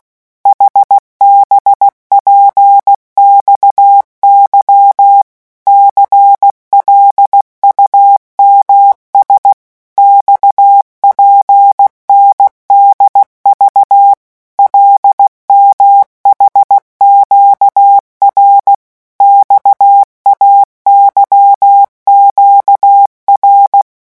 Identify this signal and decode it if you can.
Morse code, heard as a keyed tone, about 16 wpm, 800 Hz. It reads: HBPXY CLUMS XPNDV LMHQR XAYQR
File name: morse mp3 / revision5-vitesse 15 mots